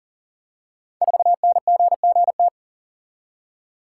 Morse code, heard as a keyed tone, about 41 wpm, 700 Hz. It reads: 4NGGT
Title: Code Groups - 5 Letter:Numbers - 40wpm